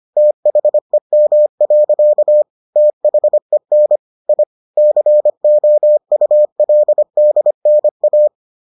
Morse code, heard as a keyed tone, about 25 wpm, 600 Hz. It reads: THEM. THEN I COULDNA